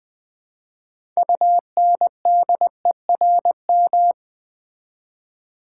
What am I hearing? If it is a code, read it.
Morse code, heard as a keyed tone, about 20 wpm, 700 Hz. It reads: UNDERM
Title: Build A Word - 3 Letters At A Time - 9-Letter Words - 20wpm